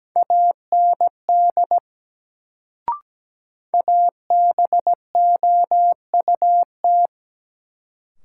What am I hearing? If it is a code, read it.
Morse code, heard as a keyed tone, about 17 wpm, 700 Hz. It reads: AND ABOUT
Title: Sentences from Top 100 Words - New Word Drill - 17wpm